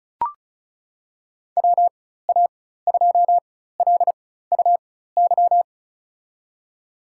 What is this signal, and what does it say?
Morse code, heard as a keyed tone, about 34 wpm, spaced as for 16 wpm, 700 Hz. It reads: WA2LUY
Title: USA Call Signs - 6 Characters - 4x Character Spacing - 35wpm